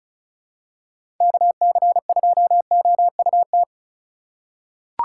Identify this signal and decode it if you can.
Morse code, heard as a keyed tone, about 36 wpm, 700 Hz. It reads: KC2OUT
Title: USA Call Signs - 6 Characters - 3x Character Spacing - 35wpm